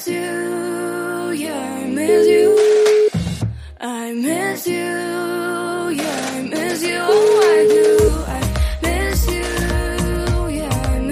Ringback tone (Europe, 425 Hz), with noise.